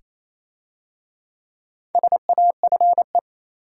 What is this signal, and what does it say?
Morse code, heard as a keyed tone, about 27 wpm, 690 Hz. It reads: SAFE
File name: Easy Words - Speed-Racing with Hints - 22wpm